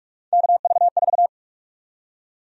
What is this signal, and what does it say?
Morse code, heard as a keyed tone, about 45 wpm, 700 Hz. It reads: KV4
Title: USA Call Sign Prefixes - 45wpm